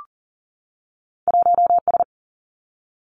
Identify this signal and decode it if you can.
Morse code, heard as a keyed tone, about 42 wpm, 700 Hz. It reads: 1S
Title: Two Challenging Letter-Number Combinations 40wpm